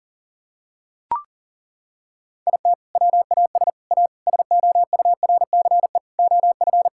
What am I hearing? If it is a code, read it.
Morse code, heard as a keyed tone, about 40 wpm, 700 Hz. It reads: IT WAS A SOURCE OF
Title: Sentences from Top 1000 Words - Review Part 5 - 40wpm